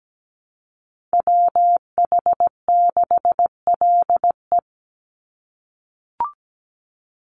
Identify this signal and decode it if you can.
Morse code, heard as a keyed tone, about 17 wpm, 700 Hz. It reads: WH6LE
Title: NA SOTA Activator Callsigns 17wpm